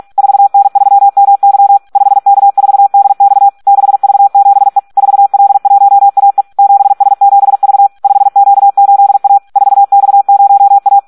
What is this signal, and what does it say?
Morse code, heard as a keyed tone, about 46 wpm, 800 Hz. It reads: 4N3MY 5K4DX 6V7E 4L1AE 7S7V 5Q8A 4X0A
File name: Audio practica CW / SD 1 46